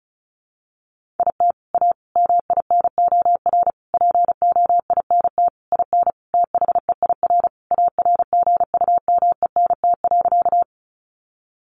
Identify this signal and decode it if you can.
Morse code, heard as a keyed tone, about 34 wpm, 700 Hz. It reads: IT A MINOR POINT IN THEIR ARGUMENT.
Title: Sentences from Top 1500 Words 35wpm